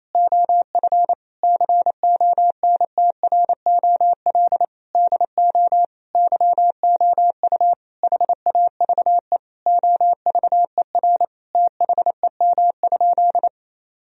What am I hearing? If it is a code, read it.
Morse code, heard as a keyed tone, about 28 wpm, 700 Hz. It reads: OF CONTROL DO YOU HAVE OVER THEM?
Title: Sentences from Top 1000 Words - Review Part 3 - 28wpm